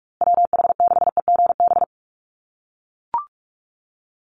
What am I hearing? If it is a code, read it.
Morse code, heard as a keyed tone, about 44 wpm, 700 Hz. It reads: WH6EGB
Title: USA Call Signs Repeated 2x 45wpm